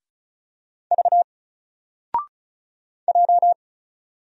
Morse code, heard as a keyed tone, about 36 wpm, 700 Hz. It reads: V J
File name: Single-Letter-Number-35wpm